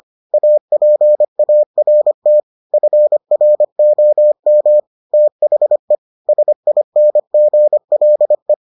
Morse code, heard as a keyed tone, about 25 wpm, 600 Hz. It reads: APART FROM THE SINGLE